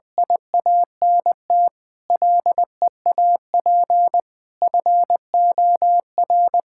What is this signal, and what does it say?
Morse code, heard as a keyed tone, about 20 wpm, 700 Hz. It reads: IANT LEAP FOR
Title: Difficult Sentences with Hints - 20wpm